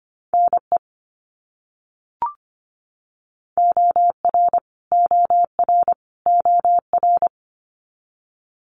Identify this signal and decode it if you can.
Morse code, heard as a keyed tone, about 25 wpm, 700 Hz. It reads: NE OR OR OR
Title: US States Repeated 3x 25wpm